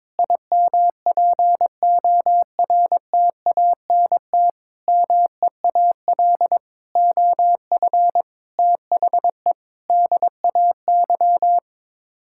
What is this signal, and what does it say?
Morse code, heard as a keyed tone, about 22 wpm, 700 Hz. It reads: IMPORTANT MEAL OF THE DAY
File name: Sets of 9 Words - 22wpm